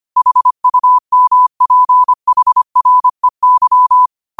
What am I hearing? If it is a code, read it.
Morse code, heard as a keyed tone, about 25 wpm, 1000 Hz. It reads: HUMPHREY
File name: qotdfiles / 25wpm